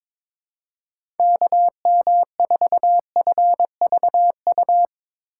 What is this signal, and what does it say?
Morse code, heard as a keyed tone, about 22 wpm, 700 Hz. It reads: KM4FVU
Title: USA Call Signs - 6 Characters - 3x Character Spacing - 22wpm